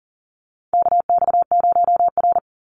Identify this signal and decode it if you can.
Morse code, heard as a keyed tone, about 41 wpm, 700 Hz. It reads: KX0R
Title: NA SOTA Chaser Callsigns 40wpm